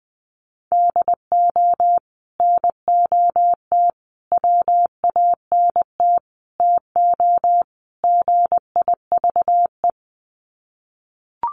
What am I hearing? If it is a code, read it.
Morse code, heard as a keyed tone, about 20 wpm, 700 Hz. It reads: DO NOT WANT TO GIVE
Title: Sets of 5 Words Encore 20wpm